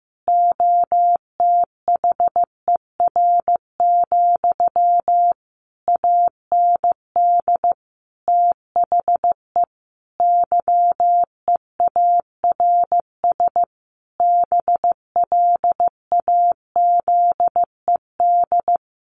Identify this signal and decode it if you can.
Morse code, heard as a keyed tone, about 15 wpm, 700 Hz. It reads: OTHER, AND THE YEARS BLAZED
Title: A Sound Of Thunder 15wpm